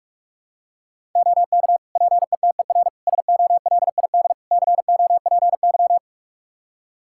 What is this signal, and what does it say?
Morse code, heard as a keyed tone, about 45 wpm, 700 Hz. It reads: OK PETER SOLID COPY
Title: K1USN SST QSO Practice 45wpm